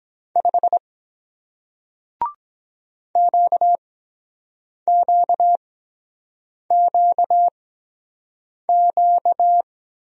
Morse code, heard as a keyed tone, about 23 wpm, spaced as for 7 wpm, 700 Hz. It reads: 5 QQQQ